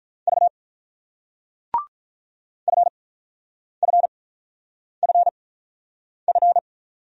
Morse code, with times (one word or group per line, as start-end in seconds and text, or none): 0.3-0.5 V
2.7-6.6 FFFF